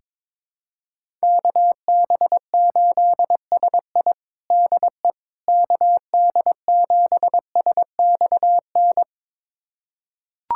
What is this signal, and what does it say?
Morse code, heard as a keyed tone, about 22 wpm, 700 Hz. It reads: KB8SI DE KD7SXN
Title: USA Callsign DE Callsign 22wpm